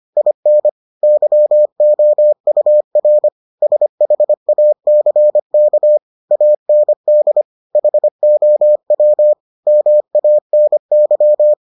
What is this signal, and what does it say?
Morse code, heard as a keyed tone, about 25 wpm, 600 Hz. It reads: IN YOUR SHACK AND HOW MANY